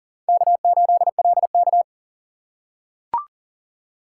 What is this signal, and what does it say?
Morse code, heard as a keyed tone, about 40 wpm, 700 Hz. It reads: K8LK
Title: USA Call Signs - 4 Characters - 4x Character Spacing - 40wpm